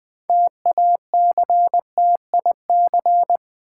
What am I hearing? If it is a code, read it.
Morse code, heard as a keyed tone, about 20 wpm, 700 Hz. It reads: TACTIC
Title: Build A Word - 2 Letters At A Time - 6-Letter Words - 20wpm